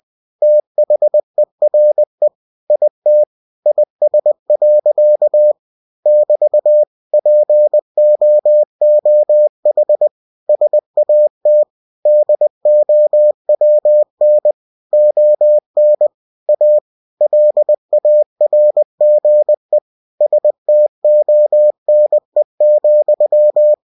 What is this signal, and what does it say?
Morse code, heard as a keyed tone, about 20 wpm, 600 Hz. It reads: THERE IT IS. <BT> POOH SAT DOWN ON A LARGE STONE,
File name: cw-books / winnie-the-pooh / winnie-20wpm-page-0045